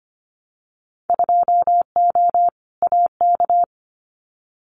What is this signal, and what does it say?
Morse code, heard as a keyed tone, about 25 wpm, 700 Hz. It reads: 2O AK
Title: Winter Field Day Exchanges Encore 25wpm